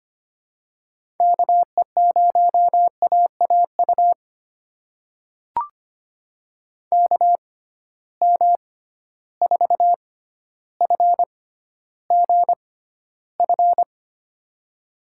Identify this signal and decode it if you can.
Morse code, heard as a keyed tone, about 25 wpm, spaced as for 13 wpm, 700 Hz. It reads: KE0AAU KM4FGF
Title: USA Call Signs - 6 Characters - 6x Character Spacing - 25wpm